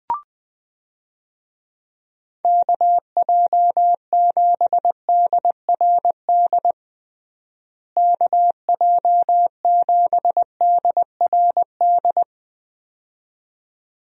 Morse code, heard as a keyed tone, about 20 wpm, 700 Hz. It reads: KJ7DRD KJ7DRD
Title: USA Call Signs - 6 Characters - Repeated 2x - 20wpm